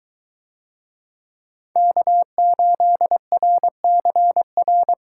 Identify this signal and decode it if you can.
Morse code, heard as a keyed tone, about 23 wpm, 700 Hz. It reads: K8RCR